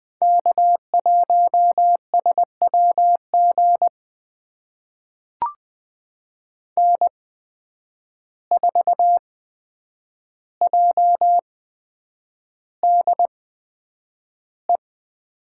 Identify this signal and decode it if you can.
Morse code, heard as a keyed tone, about 20 wpm, spaced as for 9 wpm, 700 Hz. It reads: K1SWG N4JDE
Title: USA Call Signs - 5 Characters - 8x Character Spacing - 20wpm